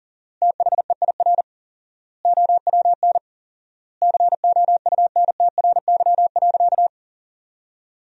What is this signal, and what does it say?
Morse code, heard as a keyed tone, about 40 wpm, 700 Hz. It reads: THEIR OWN COUNTRY.
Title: 4x Word-Spacing - Sentences from Top 300 Words - 40wpm